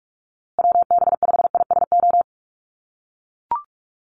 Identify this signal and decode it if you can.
Morse code, heard as a keyed tone, about 44 wpm, 690 Hz. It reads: WB5ISO
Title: USA Call Signs Repeated 3x 45wpm